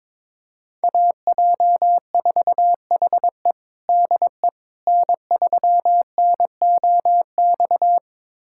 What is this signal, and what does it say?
Morse code, heard as a keyed tone, about 22 wpm, 700 Hz. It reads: AJ4HE DE N3NOX
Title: USA Callsign DE Callsign - 5 Characters - 22wpm